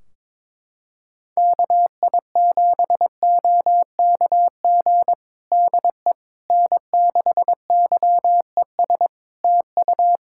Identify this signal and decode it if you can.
Morse code, heard as a keyed tone, about 22 wpm, 700 Hz. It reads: KI7OKG DE N6YES TU
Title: Practice QSO - Exchange 4 - Ending - 22wpm